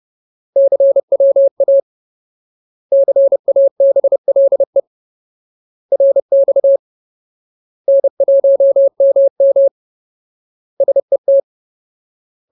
Morse code, heard as a keyed tone, about 30 wpm, 550 Hz. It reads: CWA CABLE RX N1MM SET
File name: Ultimate Word List / 30WPM Ultimate